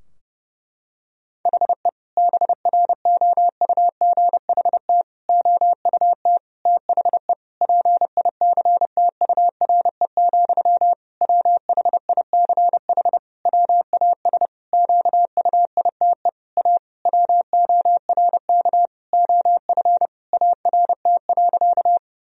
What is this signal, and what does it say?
Morse code, heard as a keyed tone, about 32 wpm, 700 Hz. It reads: HE BROUGHT OUT THE PICTURE, WHICH WAS QUITE A WORK OF ART.
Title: Sentences from Top 1000 Words - Review Part 3 - 30wpm